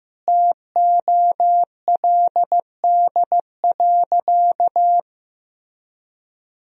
Morse code, heard as a keyed tone, about 16 wpm, 700 Hz. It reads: TOLD.
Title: Sentences from Top 400 Words and Abbreviations 15wpm